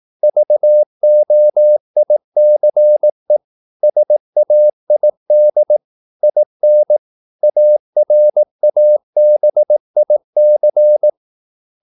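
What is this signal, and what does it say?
Morse code, heard as a keyed tone, about 18 wpm, 600 Hz. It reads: VOICE SAID IN ARABIC
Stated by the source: Morse practice at 18 wpm, English text, from CW med Gnister.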